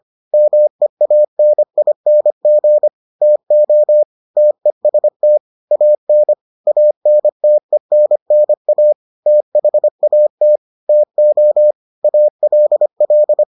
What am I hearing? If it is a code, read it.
Morse code, heard as a keyed tone, about 25 wpm, 600 Hz. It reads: MEANING TO TEST AN ANTENNA THAT TO ALL